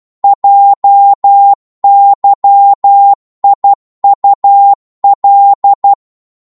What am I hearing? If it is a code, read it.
Morse code, heard as a keyed tone, about 12 wpm, 800 Hz. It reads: JYIUL
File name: Morse / ltr-12wpm